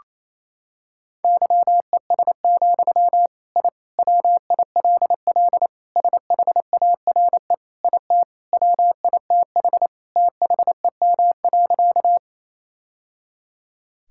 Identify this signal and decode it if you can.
Morse code, heard as a keyed tone, about 28 wpm, 700 Hz. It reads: YES, I WILL SHARE IT WITH THEM.
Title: Sentences from Top 1000 Words - Review Part 6 - 28wpm